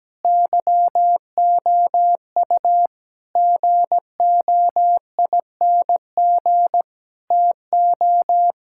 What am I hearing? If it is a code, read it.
Morse code, heard as a keyed tone, about 17 wpm, 700 Hz. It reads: YOU GOING TO
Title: Sentences from Top 1600 Words 17wpm